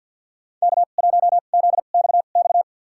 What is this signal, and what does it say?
Morse code, heard as a keyed tone, about 50 wpm, spaced as for 36 wpm, 700 Hz. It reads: K1ZXX
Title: USA Call Signs - 5 Characters - 2x Character Spacing - 50wpm